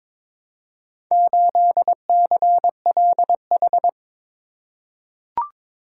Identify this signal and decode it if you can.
Morse code, heard as a keyed tone, about 22 wpm, 700 Hz. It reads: 8CLH
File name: practice / 22wpm / Four Letter-Number Combinations - 22wpm